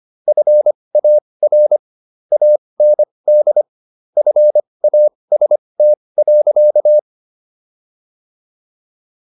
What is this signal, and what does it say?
Morse code, heard as a keyed tone, about 25 wpm, spaced as for 20 wpm, 600 Hz. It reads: FAR AND FAST.